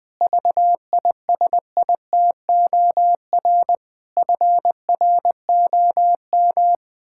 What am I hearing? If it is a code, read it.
Morse code, heard as a keyed tone, about 20 wpm, 700 Hz. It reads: VISITOR FROM
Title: Pangram Challenge v2 - 20wpm